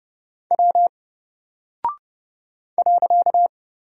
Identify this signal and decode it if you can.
Morse code, heard as a keyed tone, about 30 wpm, 700 Hz. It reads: W .
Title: Single Character (Letter:Number:Punctuation:Prosign) Alt 30wpm